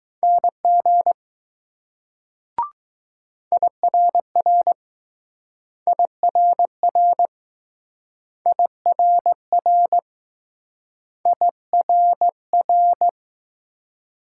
Morse code, heard as a keyed tone, about 20 wpm, 700 Hz. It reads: NG IRR IRR IRR IRR